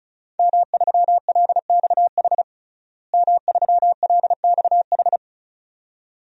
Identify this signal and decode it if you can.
Morse code, heard as a keyed tone, about 36 wpm, 700 Hz. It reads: M3LXH M3LXH
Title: DXCC-100 Call Signs - 5 Characters - Repeated 2x - 35wpm